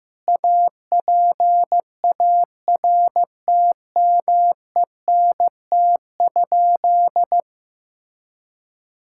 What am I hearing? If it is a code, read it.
Morse code, heard as a keyed tone, about 16 wpm, 700 Hz. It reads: APARTMENT?
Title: Sentences from Top 1000 Words - Review Part 6 - 15wpm